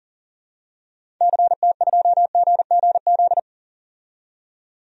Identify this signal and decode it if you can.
Morse code, heard as a keyed tone, about 37 wpm, 700 Hz. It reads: CT2GGZ
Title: DXCC-100 Call Signs - 6 Characters - 40wpm